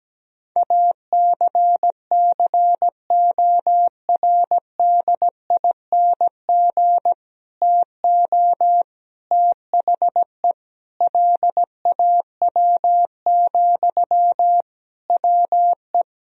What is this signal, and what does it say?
Morse code, heard as a keyed tone, about 17 wpm, 700 Hz. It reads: ACCORDING TO THE LAW, WE
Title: Sentences from Top 1000 Words - Review Part 4 - 17wpm